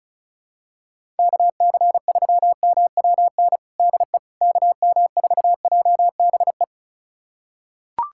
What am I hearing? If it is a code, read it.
Morse code, heard as a keyed tone, about 34 wpm, 700 Hz. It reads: KC3MWN DE KM4JBE
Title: USA Callsign DE Callsign - 6 Characters - 35wpm